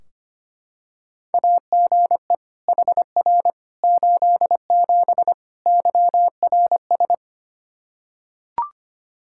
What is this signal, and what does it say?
Morse code, heard as a keyed tone, about 25 wpm, 700 Hz. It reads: AGE HR 87 YRS
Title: Practice QSO - Exchange 3 - More Info - 25wpm